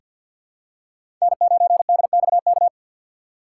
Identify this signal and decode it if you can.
Morse code, heard as a keyed tone, about 47 wpm, 700 Hz. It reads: N9DXK
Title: USA Call Signs - 5 Characters - 6x Character Spacing - 50wpm